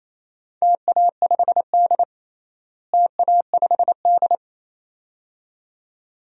Morse code, heard as a keyed tone, about 27 wpm, 700 Hz. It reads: TA5D TA5D
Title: DXCC-100 Call Signs - 4 Characters - Repeated 2x - 28wpm